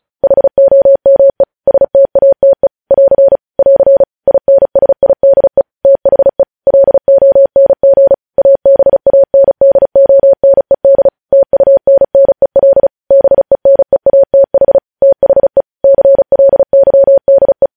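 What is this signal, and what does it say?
Morse code, heard as a keyed tone, about 36 wpm, 550 Hz. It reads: HOME STATE <AR> <AR> INSIDE THE LONG ABANDONED TUNNEL BENEATH THE CLYDE
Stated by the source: Morse code transmission of news headlines, preceded by vvv.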